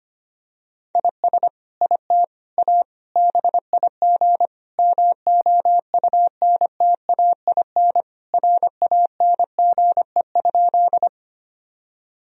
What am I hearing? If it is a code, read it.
Morse code, heard as a keyed tone, about 25 wpm, 700 Hz. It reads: IS IT A BIG MOUNTAIN RANGE?
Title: Sentences from Top 1000 Words - Review Part 4 - 25wpm